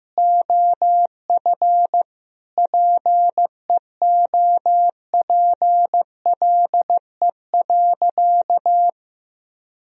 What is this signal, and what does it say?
Morse code, heard as a keyed tone, about 15 wpm, 700 Hz. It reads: OF PEOPLE.
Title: Sentences from Top 1000 Words - Review Part 3 - 15wpm